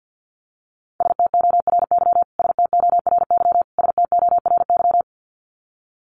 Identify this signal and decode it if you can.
Morse code, heard as a keyed tone, about 50 wpm, 700 Hz. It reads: STORY STORY STORY
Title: Top 300 Words Repeated 50wpm